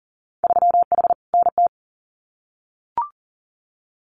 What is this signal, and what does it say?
Morse code, heard as a keyed tone, about 43 wpm, 700 Hz. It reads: 3H NT
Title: Winter Field Day Exchanges Encore 40wpm